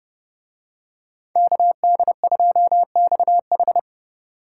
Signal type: Morse code

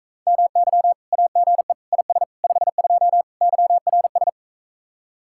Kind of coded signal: Morse code

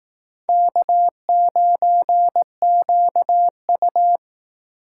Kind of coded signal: Morse code

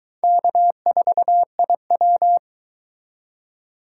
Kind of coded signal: Morse code

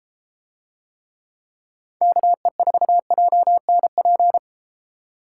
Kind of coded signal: Morse code

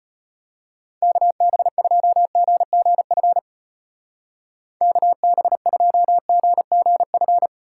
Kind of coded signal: Morse code

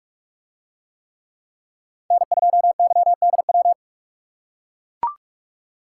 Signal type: Morse code

